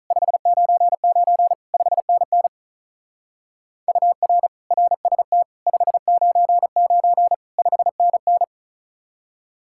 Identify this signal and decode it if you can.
Morse code, heard as a keyed tone, about 44 wpm, 700 Hz. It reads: 599 5NN UR RST 599 5NN